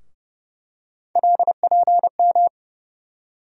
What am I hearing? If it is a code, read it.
Morse code, heard as a keyed tone, about 31 wpm, 700 Hz. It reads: LPM
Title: USA Call Sign Postfixes - 3 Letters 30wpm